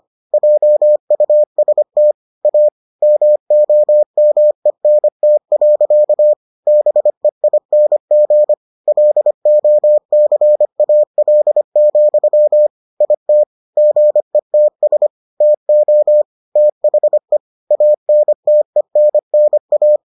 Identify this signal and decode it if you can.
Morse code, heard as a keyed tone, about 25 wpm, 600 Hz. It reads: JUST A MOMENT. BEING LOCAL, IT GETS TO THE ANTENNA